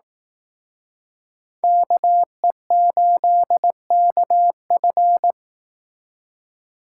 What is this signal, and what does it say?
Morse code, heard as a keyed tone, about 19 wpm, 700 Hz. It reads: KE8KF
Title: Speed Racing - USA 2x2 Call Signs 15wpm